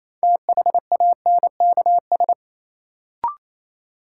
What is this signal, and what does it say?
Morse code, heard as a keyed tone, about 28 wpm, 700 Hz. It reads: THANKS
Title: Sets of 5 Words - New Word Drill - 28wpm